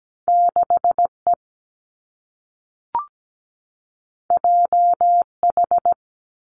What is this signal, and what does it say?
Morse code, heard as a keyed tone, about 17 wpm, 700 Hz. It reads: 6E JH
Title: Two Challenging Letter-Number Combinations 17wpm